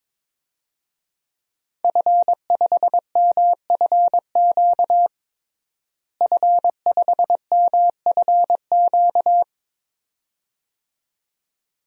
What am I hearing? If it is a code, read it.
Morse code, heard as a keyed tone, about 22 wpm, 700 Hz. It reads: F5MFQ F5MFQ
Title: DXCC-100 Call Signs - 5 Characters - Repeated 2x - 22wpm